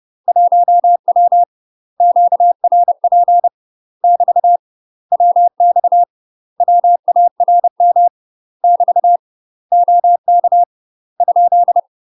Morse code, heard as a keyed tone, about 30 wpm, 700 Hz. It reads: 1W QRP <BT> WX WARM <BT> OK ?
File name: UK MASTER QSOs TRAINING / 30WPM / UK MASTER QSOs 0042